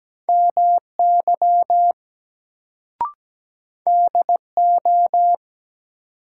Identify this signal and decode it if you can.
Morse code, heard as a keyed tone, about 17 wpm, 700 Hz. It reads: MY DO
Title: Sets of 2 Words - New Word Drill - 17wpm